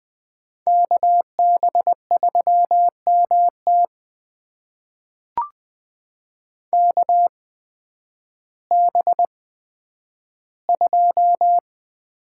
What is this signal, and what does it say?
Morse code, heard as a keyed tone, about 20 wpm, spaced as for 11 wpm, 700 Hz. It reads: KB3MT KB2
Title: USA Call Signs - 8x Character Spacing - 20wpm